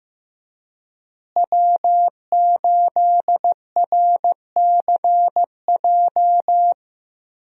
Morse code, heard as a keyed tone, about 15 wpm, 700 Hz. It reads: W8RCJ
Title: USA Call Signs - 5 Characters - Repeated 3x - 15wpm